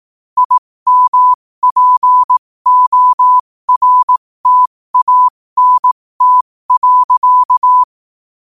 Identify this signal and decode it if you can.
Morse code, heard as a keyed tone, about 18 wpm, spaced as for 15 wpm, 1000 Hz. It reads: IMPORTANT.
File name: qotdfiles / 15wpm